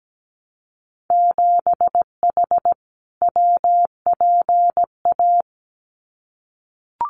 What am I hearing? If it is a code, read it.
Morse code, heard as a keyed tone, about 17 wpm, 700 Hz. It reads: 7H WPA
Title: Winter Field Day Exchanges Encore 17wpm